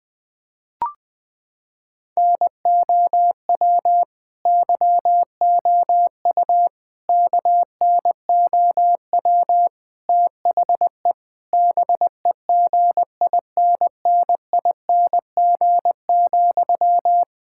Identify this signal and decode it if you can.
Morse code, heard as a keyed tone, about 20 wpm, 700 Hz. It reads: NOW YOU KNOW THE BEGINNING,
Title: Sentences from Top 1000 Words - Review Part 6 - 20wpm